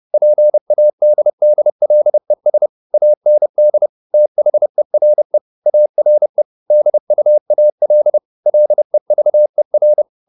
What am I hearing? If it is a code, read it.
Morse code, heard as a keyed tone, about 30 wpm, 600 Hz. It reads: PADDLES AND THERE ARE DUAL LEVER